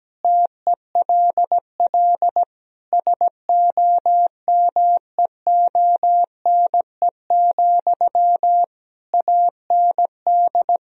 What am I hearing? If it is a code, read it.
Morse code, heard as a keyed tone, about 17 wpm, 700 Hz. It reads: TELL SOMEONE, AND
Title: Sentences from Top 1000 Words - Review Part 4 - 17wpm